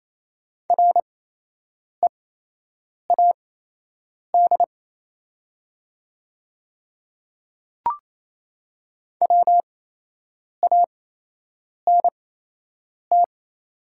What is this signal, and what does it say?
Morse code, heard as a keyed tone, about 28 wpm, spaced as for 8 wpm, 700 Hz. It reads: READ WANT
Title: 8x Character Spacing - Top 200 Words - 28wpm